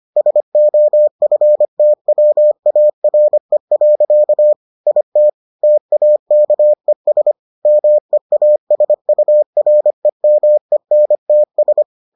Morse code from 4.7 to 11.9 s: IT TAKES MEASUREMENTS

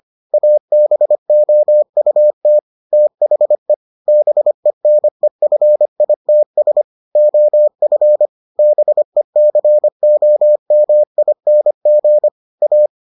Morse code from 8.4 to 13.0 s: BECOMING A

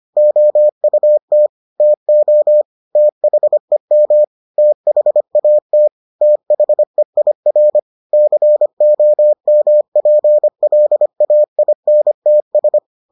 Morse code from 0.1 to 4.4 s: OUT TO THEM

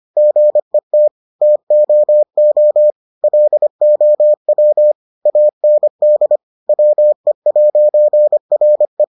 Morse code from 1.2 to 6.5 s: TOO LOW AND

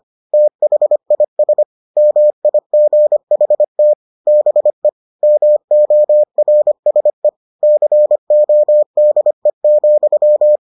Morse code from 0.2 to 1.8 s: THIS

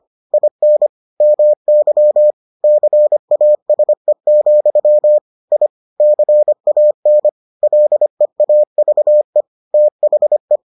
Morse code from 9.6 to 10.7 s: THE

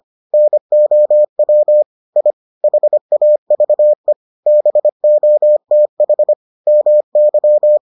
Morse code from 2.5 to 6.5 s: HAVE BOTH

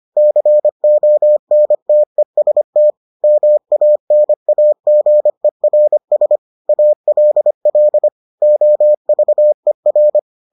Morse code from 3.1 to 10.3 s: MANAGERS ALL OVER

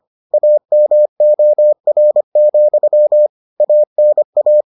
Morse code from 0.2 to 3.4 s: AMOR,

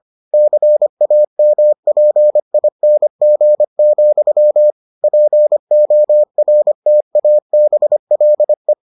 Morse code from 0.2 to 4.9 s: CAMPING,

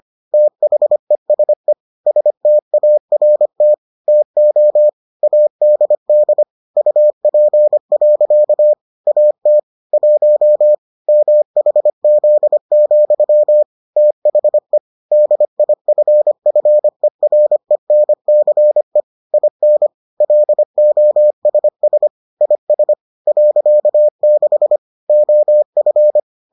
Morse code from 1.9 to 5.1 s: START TO

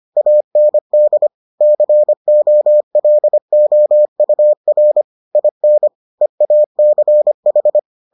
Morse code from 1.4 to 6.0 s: COLOUR IN

